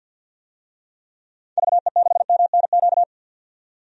Morse code, heard as a keyed tone, about 51 wpm, 700 Hz. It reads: VE6MNQ
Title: DXCC-100 Call Signs - 6 Characters - 50wpm